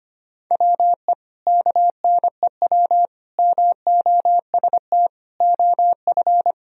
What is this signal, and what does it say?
Morse code, heard as a keyed tone, about 25 wpm, 700 Hz. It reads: WE KNEW MOST OF
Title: Sentences from Top 1000 Words - Review Part 1 - 25wpm